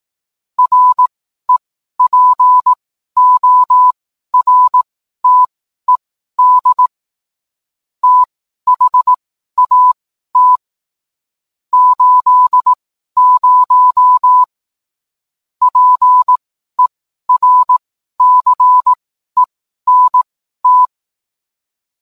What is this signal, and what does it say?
Morse code, heard as a keyed tone, about 18 wpm, spaced as for 12 wpm, 1000 Hz. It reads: REPORTED THAT 80 PERCENT